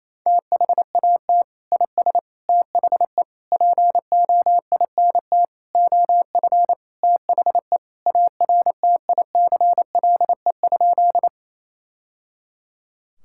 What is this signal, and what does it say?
Morse code, heard as a keyed tone, about 28 wpm, 700 Hz. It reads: THAT IS THE POINT OF THE ARTICLE?
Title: Sentences from Top 1000 Words - Review Part 5 - 28wpm